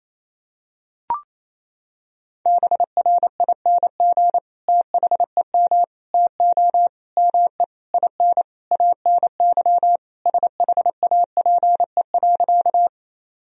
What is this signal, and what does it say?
Morse code, heard as a keyed tone, about 28 wpm, 700 Hz. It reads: BRING THEM TO ME IN ANY SHAPE.
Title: Sentences from Top 1000 Words - Review Part 3 - 28wpm